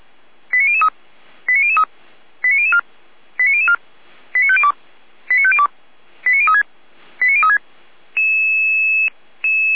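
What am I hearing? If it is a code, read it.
Five-tone selective call (ZVEI), heard as quick runs of five tones, tones of about 70 ms. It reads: 89003 89003 89004 89004 89502 89502 89036 89036
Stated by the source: ZVEI five-tone selective call received with an SDR, Feuerwehr Orlamünde fire brigade.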